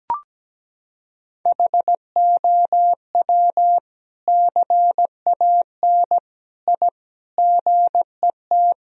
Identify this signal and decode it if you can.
Morse code, heard as a keyed tone, about 17 wpm, 700 Hz. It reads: HOW CAN I GET
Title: Sentences from Top 1000 Words - Review Part 3 - 17wpm